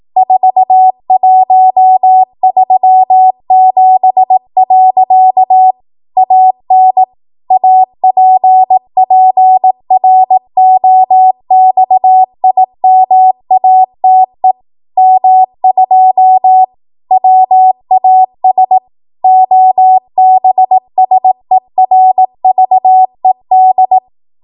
Morse code, heard as a keyed tone, about 18 wpm, 750 Hz. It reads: 4137. AN APPROXIMATE M2 WAS OBSERVED
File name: Morse / 18WPM